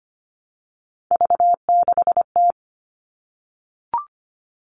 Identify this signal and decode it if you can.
Morse code, heard as a keyed tone, about 25 wpm, 700 Hz. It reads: V6T
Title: All Cut Numbers - 3-Digit 25wpm